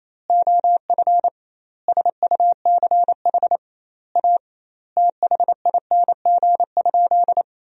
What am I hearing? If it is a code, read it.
Morse code, heard as a keyed tone, about 27 wpm, 700 Hz. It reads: OF SUCH A THING?
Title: 2x Word-Spacing - Sentences from Top 200 Words - 28wpm